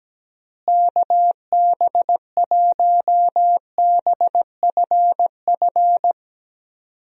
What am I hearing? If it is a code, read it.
Morse code, heard as a keyed tone, about 17 wpm, 700 Hz. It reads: KB1BFF
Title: USA Call Signs - 6 Characters - Repeated 3x - 17wpm